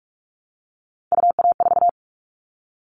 Morse code, heard as a keyed tone, about 46 wpm, 700 Hz. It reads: UA4
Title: Callsign Prefixes for Top 25 DXCC - 45wpm